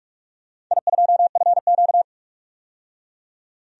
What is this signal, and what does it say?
Morse code, heard as a keyed tone, about 44 wpm, 700 Hz. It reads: I2FQ
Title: DXCC-100 Call Signs - 4 Characters - 45wpm